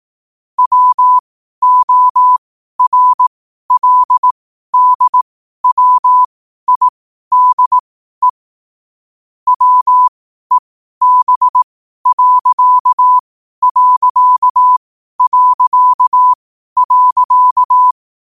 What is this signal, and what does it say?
Morse code, heard as a keyed tone, about 18 wpm, spaced as for 12 wpm, 1000 Hz. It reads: WORLDWIDE WEB....
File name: qotdfiles / 12wpm